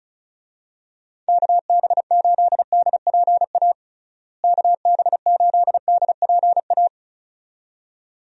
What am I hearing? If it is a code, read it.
Morse code, heard as a keyed tone, about 36 wpm, 700 Hz. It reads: KB8DPA KB8DPA
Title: USA Call Signs - 6 Characters - Repeated 2x - 35wpm